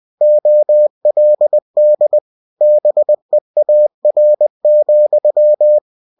Morse code, heard as a keyed tone, about 20 wpm, 600 Hz. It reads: OLD BEAR,
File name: cw-books / winnie-the-pooh / winnie-20wpm-page-0017